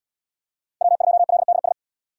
Morse code, heard as a keyed tone, about 70 wpm, 700 Hz. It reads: W2LDS